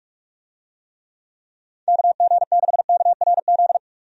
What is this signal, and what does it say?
Morse code, heard as a keyed tone, about 46 wpm, 700 Hz. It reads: KG6KRZ
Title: USA Call Signs - 6 Characters - 45wpm